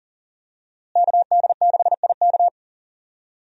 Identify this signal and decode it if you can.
Morse code, heard as a keyed tone, about 40 wpm, 700 Hz. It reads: KD6IK
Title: USA Call Signs - 5 Characters - Repeated 2x - 40wpm